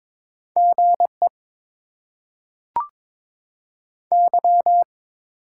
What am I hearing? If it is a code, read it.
Morse code, heard as a keyed tone, about 22 wpm, 700 Hz. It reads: GE Y
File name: Top 90 QSO Elements - 8x Character Spacing - 22wpm